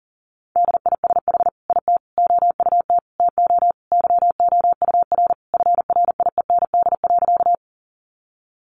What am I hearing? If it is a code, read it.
Morse code, heard as a keyed tone, about 40 wpm, 700 Hz. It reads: DISH IT OUT TO YOUR FRIEND.
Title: Sentences from Top 1300 Words 40wpm